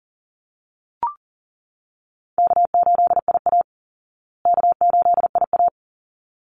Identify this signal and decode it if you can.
Morse code, heard as a keyed tone, about 40 wpm, 700 Hz. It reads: K8IA K8IA
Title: Call Signs Repeated 3x - 40wpm